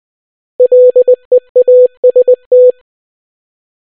Morse code, heard as a keyed tone, about 20 wpm, 500 Hz. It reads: LEAST